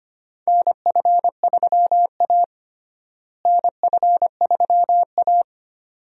Morse code, heard as a keyed tone, about 25 wpm, 700 Hz. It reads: NF3A NF3A
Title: USA Call Signs - 4 Characters - Repeated 2x - 25wpm